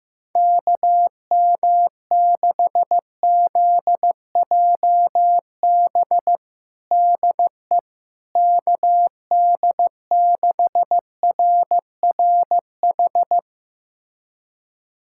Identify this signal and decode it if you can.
Morse code, heard as a keyed tone, about 15 wpm, 700 Hz. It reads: KM6ZJB DE KD6RRH
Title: USA Callsign DE Callsign 15wpm